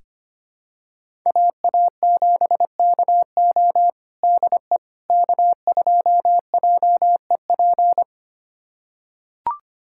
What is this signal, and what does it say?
Morse code, heard as a keyed tone, about 25 wpm, 700 Hz. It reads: AA7KO DE K2JEP
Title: USA Callsign DE Callsign - 5 Characters - 25wpm